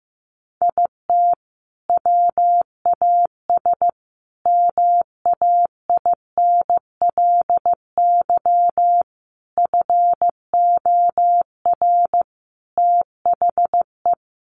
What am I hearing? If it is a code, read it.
Morse code, heard as a keyed tone, about 14 wpm, 700 Hz. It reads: IT WAS MAINLY FOR THE
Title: Sentences from Top 1400 Words 15wpm